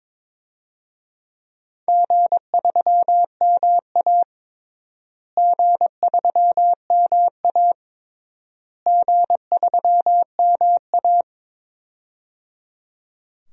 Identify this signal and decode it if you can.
Morse code, heard as a keyed tone, about 22 wpm, 700 Hz. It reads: G3MA G3MA G3MA
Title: DXCC-100 Call Signs - 4 Characters - Repeated 3x - 22wpm